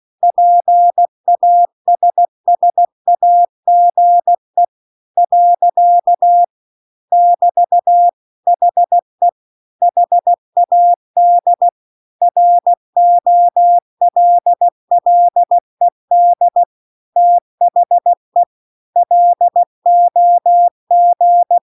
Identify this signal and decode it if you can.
Morse code, heard as a keyed tone, about 16 wpm, 700 Hz. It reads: PASSAGE . <BT> HE HAD ROLLED THE LOG
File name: CW-The-Big-Twoheaded-River-2-16WPM